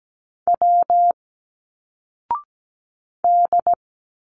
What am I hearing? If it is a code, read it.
Morse code, heard as a keyed tone, about 17 wpm, 700 Hz. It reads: W D
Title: Single Letters 17wpm